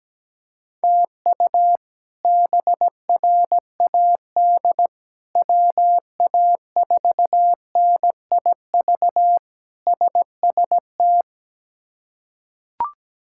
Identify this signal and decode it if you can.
Morse code, heard as a keyed tone, about 17 wpm, 700 Hz. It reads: TU BRAD WA4NIV SST
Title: K1USN SST QSO Practice - 17wpm